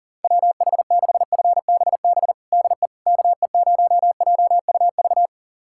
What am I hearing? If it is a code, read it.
Morse code, heard as a keyed tone, about 40 wpm, 700 Hz. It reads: WH6FBB DE KE0JUV
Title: USA Callsign DE Callsign 40wpm